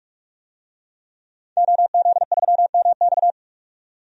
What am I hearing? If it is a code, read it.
Morse code, heard as a keyed tone, about 46 wpm, 700 Hz. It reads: OZ3MX
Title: DXCC-100 Call Signs - 5 Characters - 45wpm